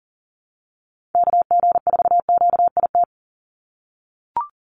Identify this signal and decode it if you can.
Morse code, heard as a keyed tone, about 40 wpm, 690 Hz. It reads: KG4QIT
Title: USA Call Signs 40wpm